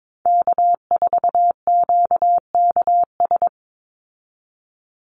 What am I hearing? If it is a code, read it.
Morse code, heard as a keyed tone, about 22 wpm, 700 Hz. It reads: K4QKS
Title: USA Call Signs 22wpm